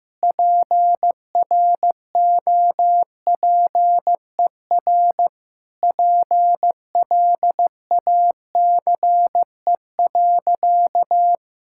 Morse code, heard as a keyed tone, about 16 wpm, 700 Hz. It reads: PROPER PLACE.
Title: Sentences from Top 1000 Words - Review Part 5 - 15wpm